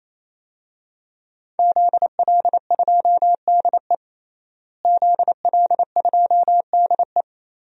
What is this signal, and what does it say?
Morse code, heard as a keyed tone, about 28 wpm, 700 Hz. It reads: ZL2DE ZL2DE
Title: DXCC-100 Call Signs - 5 Characters - Repeated 3x - 28wpm